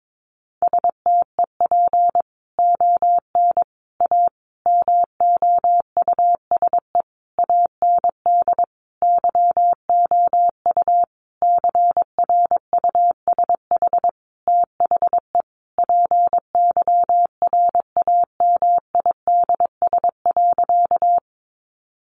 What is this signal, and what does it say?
Morse code, heard as a keyed tone, about 22 wpm, 700 Hz. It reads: STEP ON A MOUSE AND YOU CRUSH THE PYRAMIDS.
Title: A Sound Of Thunder 22wpm